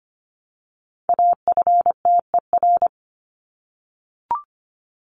Morse code, heard as a keyed tone, about 25 wpm, 700 Hz. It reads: AFTER
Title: Top 200 Words 25wpm